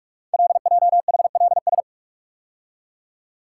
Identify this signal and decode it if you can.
Morse code, heard as a keyed tone, about 44 wpm, 690 Hz. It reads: LJHLS
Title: Code Groups - 5 Letters - 45wpm